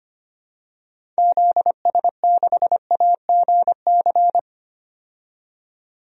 Morse code, heard as a keyed tone, about 25 wpm, 700 Hz. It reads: ZS6AGC
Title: DXCC-100 Call Signs - 6 Characters - 25wpm